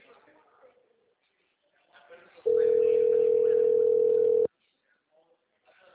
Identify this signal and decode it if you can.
Ringback tone (North America), 440 + 480 Hz, with speech.
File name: out- -2005- - - .67131